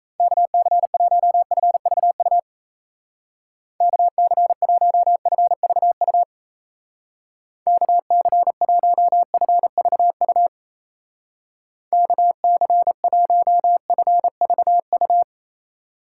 Morse code, heard as a keyed tone, about 35 wpm, 700 Hz. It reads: KC1FVU KC1FVU KC1FVU KC1FVU
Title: Speed Racing - USA 2x3 Call Signs 28wpm